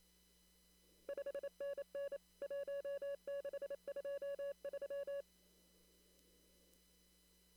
Morse code, heard as a keyed tone, about 27 wpm, 560 Hz. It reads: 5NN 1623
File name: CW / kt5j wpx cw-op-k5pi